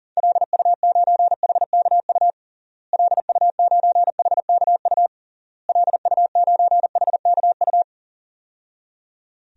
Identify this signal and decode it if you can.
Morse code, heard as a keyed tone, about 40 wpm, 700 Hz. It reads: LU9HKU LU9HKU LU9HKU
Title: DXCC-100 Call Signs - 6 Characters - Repeated 3x - 40wpm